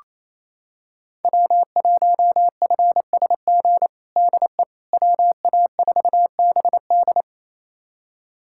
Morse code, heard as a keyed tone, about 28 wpm, 700 Hz. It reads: W1FSG DE WA4BD
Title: USA Callsign DE Callsign - 5 Characters - 28wpm